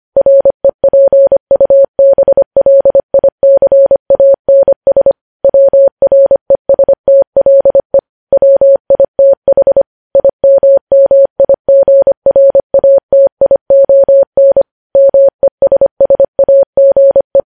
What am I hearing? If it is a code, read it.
Morse code, heard as a keyed tone, about 25 wpm, 550 Hz. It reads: REPUBLICANS WRESTLE WITH IMMIGRATION MESSAGE
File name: podcast / News headlines-1x-25x25